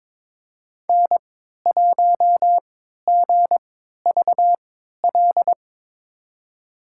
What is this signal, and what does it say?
Morse code, heard as a keyed tone, about 22 wpm, spaced as for 13 wpm, 700 Hz. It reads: N1GVL
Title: USA Call Signs - 5 Characters - 3x Character Spacing - 22wpm